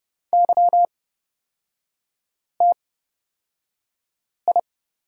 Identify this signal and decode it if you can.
Morse code, heard as a keyed tone, about 31 wpm, spaced as for 5 wpm, 690 Hz. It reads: YTI
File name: course / 30:10wpm / 236 Lesson 38e - Letter q - Three Characters 30wpm